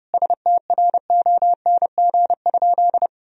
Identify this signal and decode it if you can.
Morse code, heard as a keyed tone, about 32 wpm, 700 Hz. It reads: STRONG?
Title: Sentences from Top 1000 Words - Review Part 3 - 30wpm